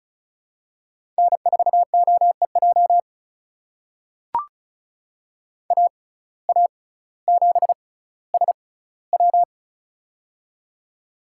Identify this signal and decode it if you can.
Morse code, heard as a keyed tone, about 36 wpm, spaced as for 18 wpm, 700 Hz. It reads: N4OEJ AA7SW
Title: USA Call Signs - 5 Characters - 6x Character Spacing - 35wpm